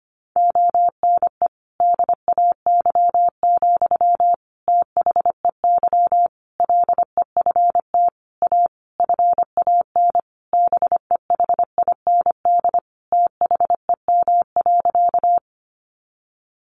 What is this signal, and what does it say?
Morse code, heard as a keyed tone, about 25 wpm, 700 Hz. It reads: ONE DAY, THEY LEFT A FAN BEHIND THEM.
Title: Sentences from Top 1300 Words 25wpm